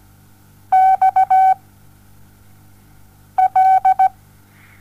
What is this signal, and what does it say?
Morse code, heard as a keyed tone, about 17 wpm, 760 Hz. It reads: X L
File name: Onlinelehrgang / morsen / DJ4UF Morsen07